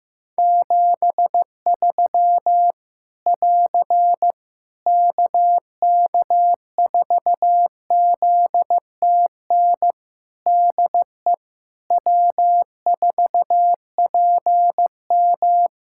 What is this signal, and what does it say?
Morse code, heard as a keyed tone, about 15 wpm, 700 Hz. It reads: 73 <AR> KK4ZTN DE W4PM
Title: Practice QSO - Exchange 4 - Ending - 15wpm